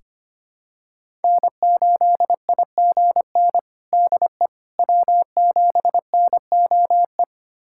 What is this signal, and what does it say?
Morse code, heard as a keyed tone, about 25 wpm, 700 Hz. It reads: N8IGN DE W7NOE
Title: USA Callsign DE Callsign - 5 Characters - 25wpm